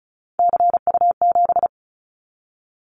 Morse code, heard as a keyed tone, about 34 wpm, 700 Hz. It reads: CU7
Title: Callsign Prefixes for Top 100 DXCC - 35wpm